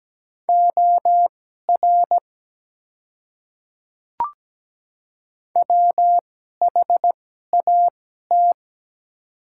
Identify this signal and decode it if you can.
Morse code, heard as a keyed tone, about 17 wpm, spaced as for 12 wpm, 700 Hz. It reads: OR WHAT
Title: 2x Character Spacing - Top 300 Words - 17wpm